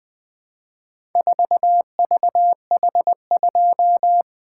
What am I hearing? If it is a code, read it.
Morse code, heard as a keyed tone, about 20 wpm, 700 Hz. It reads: 4VH2
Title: Four Letter-Number Combinations - 20wpm